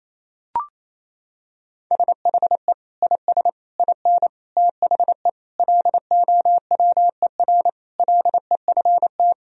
Morse code, heard as a keyed tone, about 28 wpm, 700 Hz. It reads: SHE IS IN THE LOWER LEFT
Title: Sentences from Top 1000 Words - Review Part 4 - 28wpm